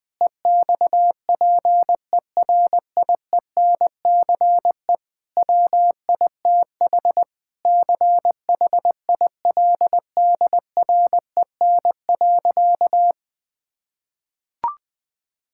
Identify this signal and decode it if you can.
Morse code, heard as a keyed tone, about 21 wpm, 700 Hz. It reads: EXPERIENCE WITH CHILDREN.
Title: Sentences from Top 1000 Words - Review Part 5 - 20wpm